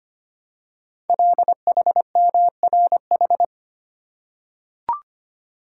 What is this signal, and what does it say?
Morse code, heard as a keyed tone, about 25 wpm, 700 Hz. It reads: LHMRH
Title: Code Groups - 5 Letters - 25wpm